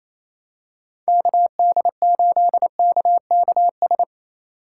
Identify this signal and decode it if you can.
Morse code, heard as a keyed tone, about 28 wpm, 700 Hz. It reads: KD8KKS
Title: USA Call Signs - 6 Characters - 8x Character Spacing - 28wpm